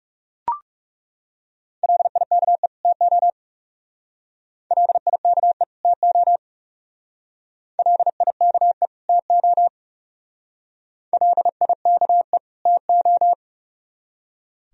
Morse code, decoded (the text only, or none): LIKE TO LIKE TO LIKE TO LIKE TO